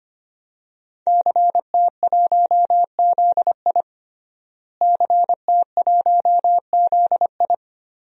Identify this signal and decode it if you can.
Morse code, heard as a keyed tone, about 25 wpm, 700 Hz. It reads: CT1ZI CT1ZI
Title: DXCC-100 Call Signs - 5 Characters - Repeated 2x - 25wpm